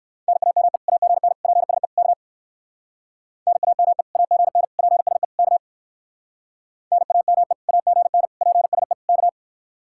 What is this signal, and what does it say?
Morse code, heard as a keyed tone, about 59 wpm, 700 Hz. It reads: NAME AGN PSE K NAME AGN PSE K NAME AGN PSE K